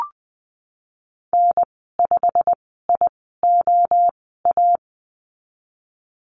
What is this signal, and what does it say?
Morse code, heard as a keyed tone, about 20 wpm, spaced as for 15 wpm, 700 Hz. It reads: N5IOA
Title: 088 Lesson 15g - Number 5 - Call Signs 20wpm